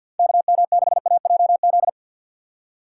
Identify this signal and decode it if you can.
Morse code, heard as a keyed tone, about 53 wpm, 700 Hz. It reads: KM6AJZ
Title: USA Call Signs - 6 Characters - 4x Character Spacing - 50wpm